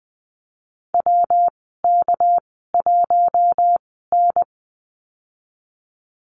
Morse code, heard as a keyed tone, about 20 wpm, spaced as for 15 wpm, 700 Hz. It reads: WK1N
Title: 238 Lesson 38g - Letter q - Call Signs 20wpm